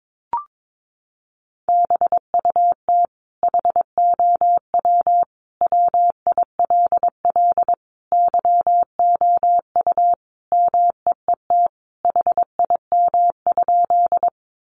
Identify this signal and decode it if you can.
Morse code, heard as a keyed tone, about 22 wpm, 700 Hz. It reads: BUT HOW WILL YOU MEET HIM?
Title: Sentences from Top 1100 Words 22wpm